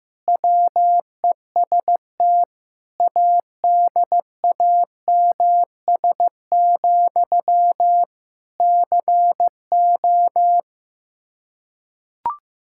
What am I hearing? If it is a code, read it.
Morse code, heard as a keyed tone, about 15 wpm, 700 Hz. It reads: WEST ADAMS, CO
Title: Practice QSO - Exchange 1 - Beginning - 15wpm